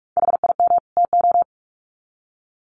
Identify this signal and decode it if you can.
Morse code, heard as a keyed tone, about 44 wpm, 700 Hz. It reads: HIM TO
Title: Sets of 2 Words 45wpm